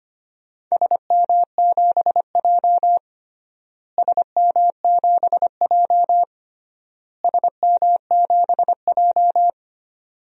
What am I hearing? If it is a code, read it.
Morse code, heard as a keyed tone, about 25 wpm, 700 Hz. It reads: SM7J SM7J SM7J
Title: DXCC-100 Call Signs - 4 Characters - Repeated 3x - 25wpm